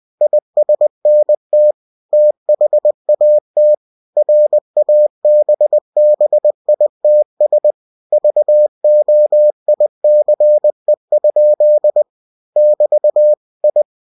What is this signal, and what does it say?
Morse code, heard as a keyed tone, about 20 wpm, 600 Hz. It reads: ISNT THAT RABBITS VOICE? <BT> I
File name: cw-books / winnie-the-pooh / winnie-20wpm-page-0014